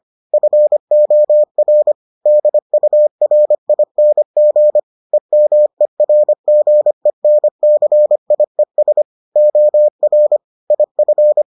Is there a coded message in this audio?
Yes: Morse code from 0.3 to 11.4 s, heard as a keyed tone, about 25 wpm, 600 Hz. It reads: FOR DURING EMERGENCIES OR IF